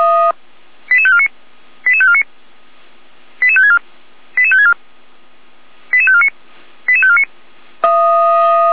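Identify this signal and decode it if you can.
Five-tone selective call (ZVEI), heard as quick runs of five tones, tones of about 70 ms. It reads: 88549 88549 88564 88564 88549 88549
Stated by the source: ZVEI five-tone call 88389 received by SDR.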